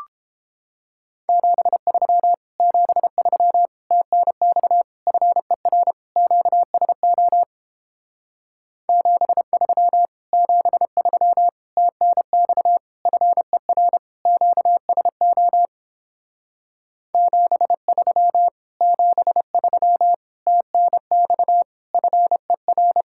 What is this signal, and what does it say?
Morse code, heard as a keyed tone, about 29 wpm, 700 Hz. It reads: 73 73 TNX FER QSO 73 73 TNX FER QSO 73 73 TNX FER